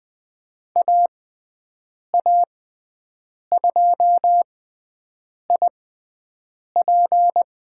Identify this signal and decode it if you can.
Morse code, heard as a keyed tone, about 20 wpm, spaced as for 7 wpm, 700 Hz. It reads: AA2IP
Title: USA Call Signs - 5 Characters - 6x Character Spacing - 20wpm